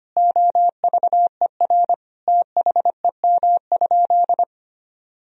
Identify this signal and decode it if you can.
Morse code, heard as a keyed tone, about 25 wpm, 700 Hz. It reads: OVER THEM?
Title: Sentences from Top 1000 Words - Review Part 3 - 25wpm